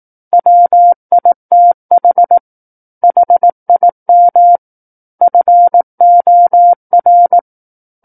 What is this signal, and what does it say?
Morse code, heard as a keyed tone, about 18 wpm, 700 Hz. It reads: WITH HIM FOR
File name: 18 WPM